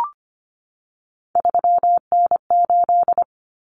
Morse code, heard as a keyed tone, about 25 wpm, 700 Hz. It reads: 3N8
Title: Common Cut Numbers - 3-Digit 25wpm